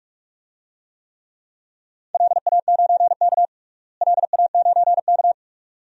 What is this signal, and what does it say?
Morse code, heard as a keyed tone, about 41 wpm, 700 Hz. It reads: LA9K LA9K
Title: DXCC-100 Call Signs - 4 Characters - Repeated 2x - 45wpm